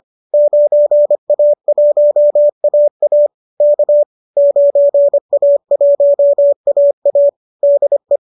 Morse code, heard as a keyed tone, about 25 wpm, 570 Hz. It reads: 9A1AA K 9A1AA DE
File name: qso219 25